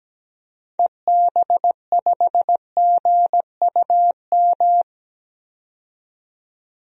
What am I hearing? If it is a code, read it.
Morse code, heard as a keyed tone, about 17 wpm, 700 Hz. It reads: EB5GUM
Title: DXCC-100 Call Signs - 6 Characters - 17wpm